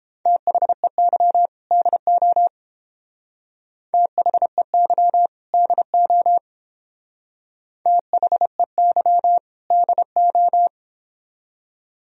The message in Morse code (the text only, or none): THEY DO THEY DO THEY DO